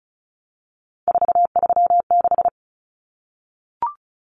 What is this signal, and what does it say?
Morse code, heard as a keyed tone, about 34 wpm, 700 Hz. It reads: 436
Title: Common Cut Numbers - 3-Digit 35wpm